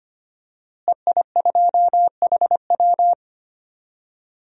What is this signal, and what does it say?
Morse code, heard as a keyed tone, about 25 wpm, 700 Hz. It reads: EI2HW
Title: DXCC-100 Call Signs - 5 Characters - Repeated 3x - 25wpm